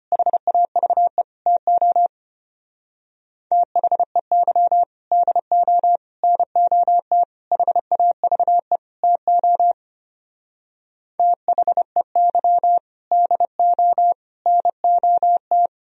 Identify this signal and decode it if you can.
Morse code, heard as a keyed tone, about 32 wpm, 700 Hz. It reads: HAVE TO THEY DO NOT HAVE TO THEY DO NOT